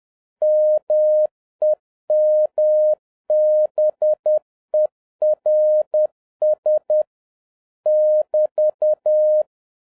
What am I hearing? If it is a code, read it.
Morse code, heard as a keyed tone, about 10 wpm, 610 Hz. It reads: MEMBERS <BT>